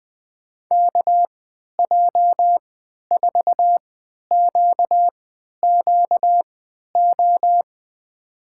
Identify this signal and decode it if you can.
Morse code, heard as a keyed tone, about 20 wpm, spaced as for 11 wpm, 700 Hz. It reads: KJ4QQO
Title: USA Call Signs - 6 Characters - 3x Character Spacing - 20wpm